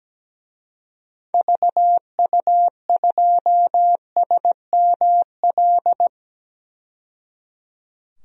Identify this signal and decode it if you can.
Morse code, heard as a keyed tone, about 17 wpm, 700 Hz. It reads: VU2SML
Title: DXCC-100 Call Signs - 6 Characters - 17wpm